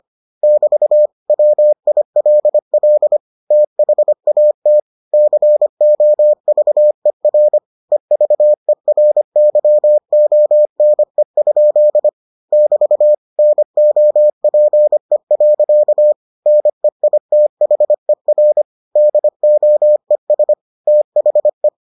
Morse code, heard as a keyed tone, about 25 wpm, 600 Hz. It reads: <BT> WILL THAT COVER EVERYONE? <BT> NOPE. NEITHER DOES THE